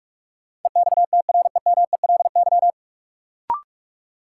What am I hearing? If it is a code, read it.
Morse code, heard as a keyed tone, about 46 wpm, 700 Hz. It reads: EXTREMELY
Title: Sentences from Top 1200 Words - New Word Drill Alt - 45wpm